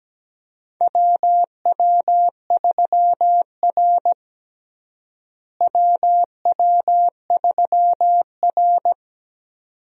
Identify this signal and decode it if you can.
Morse code, heard as a keyed tone, about 17 wpm, 700 Hz. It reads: WW3R WW3R
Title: USA Call Signs - 4 Characters - Repeated 2x - 17wpm